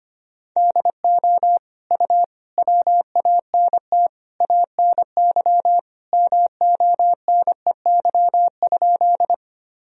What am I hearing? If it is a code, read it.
Morse code, heard as a keyed tone, about 25 wpm, 700 Hz. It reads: DO U WANT ANY MONEY?
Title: Sentences from Top 400 Words and Abbreviations 25wpm